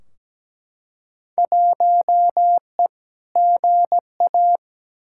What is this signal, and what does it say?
Morse code, heard as a keyed tone, about 17 wpm, 700 Hz. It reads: 1E GA
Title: Field Day Exchanges - 17wpm